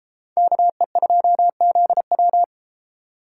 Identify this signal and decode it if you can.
Morse code, heard as a keyed tone, about 32 wpm, 700 Hz. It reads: KE2ZW